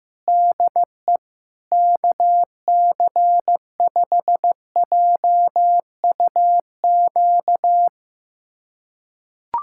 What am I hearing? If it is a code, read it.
Morse code, heard as a keyed tone, about 15 wpm, 700 Hz. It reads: DE KC5JUQ
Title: USA Callsign DE Callsign - 6 Characters - 15wpm